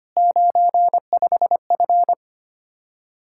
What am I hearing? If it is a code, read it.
Morse code, heard as a keyed tone, about 25 wpm, 700 Hz. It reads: 95F
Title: Practice QSO - Exchange 2 - Info - 25wpm